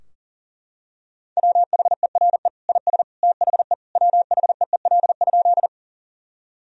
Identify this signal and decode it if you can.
Morse code, heard as a keyed tone, about 40 wpm, 700 Hz. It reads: WHERE IS THE WHEEL?
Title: Sentences from Top 1000 Words - Review Part 3 - 40wpm